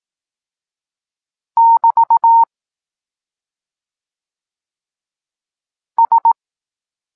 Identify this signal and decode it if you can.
Morse code, heard as a keyed tone, about 18 wpm, 920 Hz. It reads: <BT> S
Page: Tekst 14 – doskonalenie odbioru poznanych liter